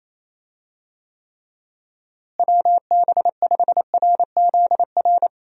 Morse code, heard as a keyed tone, about 28 wpm, 700 Hz. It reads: WB5RZR
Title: USA Call Signs - 6 Characters - Repeated 3x - 28wpm